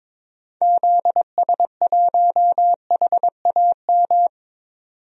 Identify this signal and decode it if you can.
Morse code, heard as a keyed tone, about 21 wpm, 700 Hz. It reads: ZS1HAM
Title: DXCC-100 Call Signs - 6 Characters - 22wpm